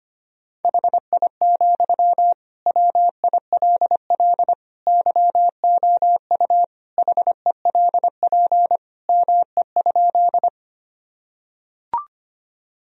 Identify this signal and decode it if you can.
Morse code, heard as a keyed tone, about 25 wpm, 700 Hz. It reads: HI, WILL YOU HELP ME?
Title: Sentences from Top 1000 Words - Review Part 3 - 25wpm